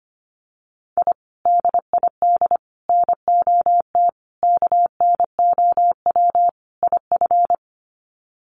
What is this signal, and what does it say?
Morse code, heard as a keyed tone, about 25 wpm, 700 Hz. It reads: I DID NOT KNOW IF
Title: Sets of 5 Words Encore 25wpm